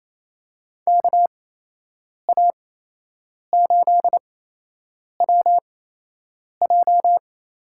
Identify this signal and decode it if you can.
Morse code, heard as a keyed tone, about 28 wpm, spaced as for 8 wpm, 700 Hz. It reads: KA8WJ
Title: USA Call Signs - 6 Characters - 8x Character Spacing - 28wpm